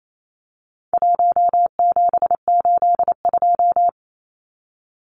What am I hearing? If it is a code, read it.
Morse code, heard as a keyed tone, about 28 wpm, 700 Hz. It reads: 1782
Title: Numbers - 4-Digit 28wpm